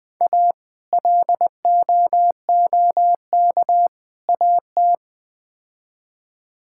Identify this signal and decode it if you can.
Morse code, heard as a keyed tone, about 20 wpm, 700 Hz. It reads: A LOOK AT
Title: Build A Phrase - 4-Word Phrases - 20wpm